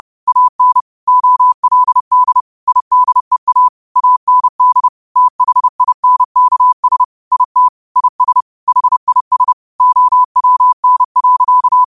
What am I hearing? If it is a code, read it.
Morse code, heard as a keyed tone, about 30 wpm, 1000 Hz. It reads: AN OLD IDEA AND THINKS IT IS HIS OWN.
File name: qotdfiles / 30wpm